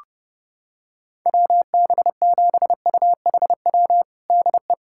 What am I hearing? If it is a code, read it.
Morse code, heard as a keyed tone, about 30 wpm, 700 Hz. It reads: WB7UHW DE
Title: USA Callsign DE Callsign 30wpm